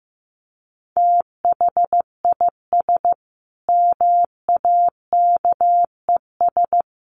Morse code, heard as a keyed tone, about 15 wpm, 700 Hz. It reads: THIS MAKES
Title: A Sound Of Thunder 15wpm